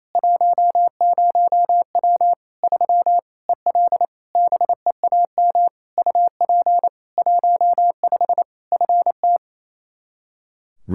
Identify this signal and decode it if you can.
Morse code, heard as a keyed tone, about 28 wpm, 700 Hz. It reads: 10W 3 EL BEAM UP 15 FT
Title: Practice QSO - Exchange 2 - Info - 28wpm